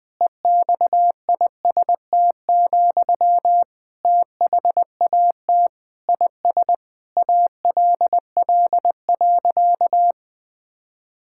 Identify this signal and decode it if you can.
Morse code, heard as a keyed tone, about 20 wpm, 700 Hz. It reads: EXIST, THAT IS ALL.
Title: Sentences from Top 1000 Words - Review Part 5 - 20wpm